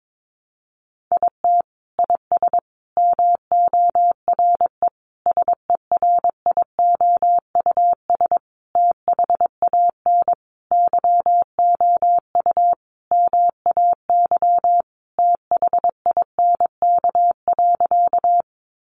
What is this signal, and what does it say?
Morse code, heard as a keyed tone, about 22 wpm, 700 Hz. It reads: IT IS MORE SERIOUS THAN YOU MAY THINK.
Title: Sentences from Top 1100 Words 22wpm